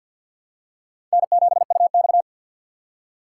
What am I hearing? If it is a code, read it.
Morse code, heard as a keyed tone, about 50 wpm, 700 Hz. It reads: N7UX
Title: USA Call Signs - 4 Characters - 3x Character Spacing - 50wpm